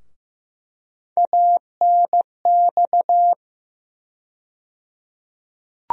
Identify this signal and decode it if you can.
Morse code, heard as a keyed tone, about 15 wpm, 700 Hz. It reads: ANX
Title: USA Call Sign Postfixes - 3 Letters 15wpm